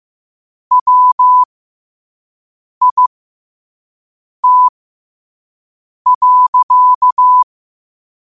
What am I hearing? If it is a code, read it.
Morse code, heard as a keyed tone, about 15 wpm, spaced as for 5 wpm, 1000 Hz. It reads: WIT.
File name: qotdfiles / 5wpm